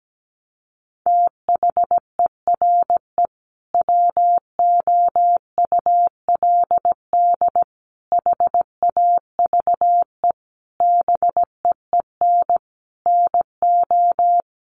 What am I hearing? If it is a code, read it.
Morse code, heard as a keyed tone, about 17 wpm, 700 Hz. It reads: THERE WOULD HAVE BEEN NO
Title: Sentences from Top 1400 Words 17wpm